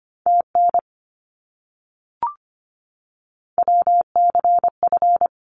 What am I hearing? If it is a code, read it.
Morse code, heard as a keyed tone, about 25 wpm, 700 Hz. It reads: TN WCF
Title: ARRL Sections 25wpm